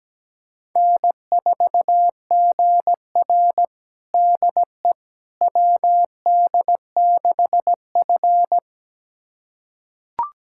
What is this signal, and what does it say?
Morse code, heard as a keyed tone, about 17 wpm, 700 Hz. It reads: N4GR DE WD6F
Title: USA Callsign DE Callsign - 4 Characters - 17wpm